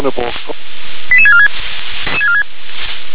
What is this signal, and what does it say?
Five-tone selective call (ZVEI), heard as quick runs of five tones, tones of about 70 ms, with speech. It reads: 88646